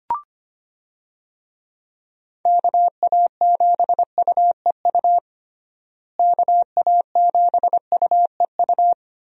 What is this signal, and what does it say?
Morse code, heard as a keyed tone, about 25 wpm, 700 Hz. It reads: KA7UEU KA7UEU
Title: USA Call Signs - 6 Characters - Repeated 2x - 25wpm